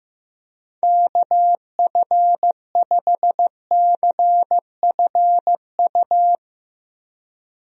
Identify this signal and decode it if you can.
Morse code, heard as a keyed tone, about 15 wpm, 700 Hz. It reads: KF5CFU
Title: USA Call Signs - 6 Characters - 15wpm